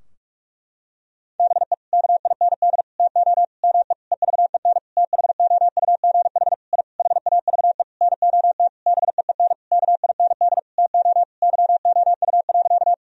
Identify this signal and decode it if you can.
Morse code, heard as a keyed tone, about 46 wpm, 700 Hz. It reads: BE KIND TO ME EVEN THOUGH I HAVE NOT BEEN KIND TO YOU.
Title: Sentences from Top 1000 Words - Review Part 2 - 45wpm